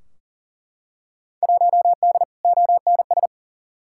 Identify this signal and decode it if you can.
Morse code, heard as a keyed tone, about 38 wpm, 700 Hz. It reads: 1D ONS
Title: Field Day Exchanges - 40wpm